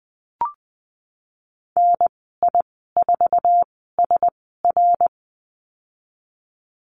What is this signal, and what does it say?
Morse code, heard as a keyed tone, about 20 wpm, spaced as for 14 wpm, 700 Hz. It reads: NI4SR
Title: 082 Lesson 14g - Number 2 - Call Signs 20wpm